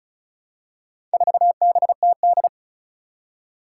Morse code, heard as a keyed tone, about 34 wpm, 700 Hz. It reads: 4BTD
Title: Four Challenging Letter-Number Combinations 35wpm